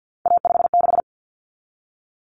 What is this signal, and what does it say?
Morse code, heard as a keyed tone, about 50 wpm, 700 Hz. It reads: A56
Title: Common Cut Numbers - 3-Digit 50wpm